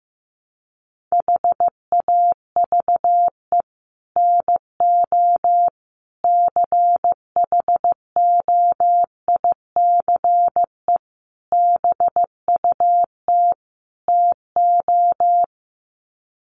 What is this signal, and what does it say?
Morse code, heard as a keyed tone, about 15 wpm, 700 Hz. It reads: HAVE NO CHOICE BUT TO
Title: Sets of 5 Words Encore 15wpm